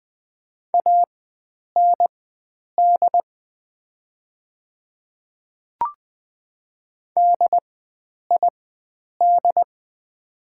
Morse code, heard as a keyed tone, about 20 wpm, spaced as for 9 wpm, 690 Hz. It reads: AND DID
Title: 4x Character Spacing - Top 100 Words - 20wpm